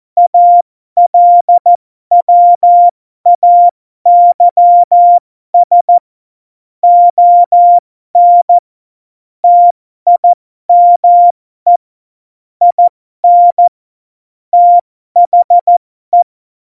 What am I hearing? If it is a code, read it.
Morse code, heard as a keyed tone, about 14 wpm, spaced as for 12 wpm, 700 Hz. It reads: ALWAYS ON TIME IN THE
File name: Morsekurs / Witze